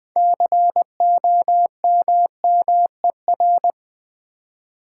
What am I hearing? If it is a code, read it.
Morse code, heard as a keyed tone, about 20 wpm, 700 Hz. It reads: COMMER
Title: Build A Word - 3 Letters At A Time - 8-Letter Words - 20wpm